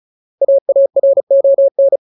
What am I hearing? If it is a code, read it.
Morse code, heard as a keyed tone, about 34 wpm, 550 Hz. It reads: AARON
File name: Ultimate Word List / 35WPM Ultimate